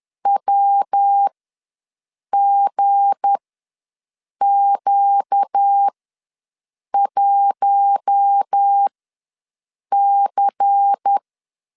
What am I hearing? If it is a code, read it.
Morse code, heard as a keyed tone, about 10 wpm, spaced as for 6 wpm, 800 Hz. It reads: WGQ1C
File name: LEZIONI / 10 SIGNS / 10C DOT ws 10wpm 4213